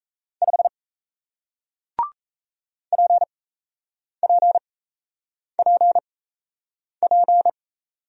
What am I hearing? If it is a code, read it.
Morse code, heard as a keyed tone, about 38 wpm, spaced as for 8 wpm, 700 Hz. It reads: 5 PPPP